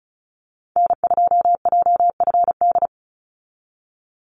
Morse code, heard as a keyed tone, about 34 wpm, 700 Hz. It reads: N2JFD
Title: Call Signs Repeated 3x - 35wpm